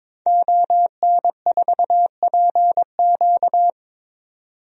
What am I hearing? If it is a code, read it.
Morse code, heard as a keyed tone, about 22 wpm, 700 Hz. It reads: ON4PQ
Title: DXCC-100 Call Signs - 5 Characters - 22wpm